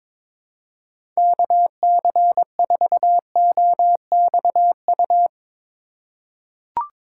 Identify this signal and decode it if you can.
Morse code, heard as a keyed tone, about 22 wpm, 700 Hz. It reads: KC4OXU
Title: USA Call Signs - 6 Characters - 2x Character Spacing - 22wpm